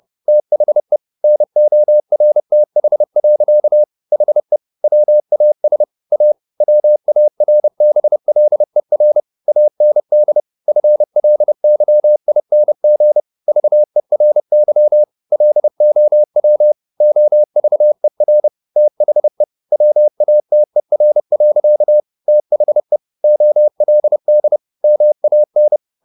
Morse code, heard as a keyed tone, about 30 wpm, 600 Hz. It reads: THE NORTH. HE WAS A WARBLER AND FLYING VERY LOW OVER THE WATER. THE OLD MAN
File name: the old man and the sea / the old man and the sea 1 30wpm